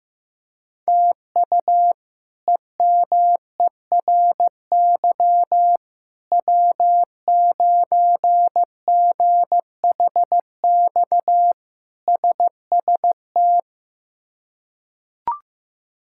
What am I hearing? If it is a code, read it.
Morse code, heard as a keyed tone, about 16 wpm, 700 Hz. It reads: TU EMERY W9GHX SST
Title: K1USN SST QSO Practice - 15wpm